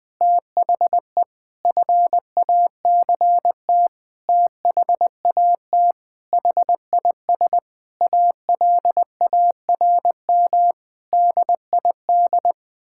Morse code, heard as a keyed tone, about 20 wpm, 700 Hz. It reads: THE FACT THAT HIS ALARM DID
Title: Sentences with 4-Word Phrases - Spoken Beginning - 20wpm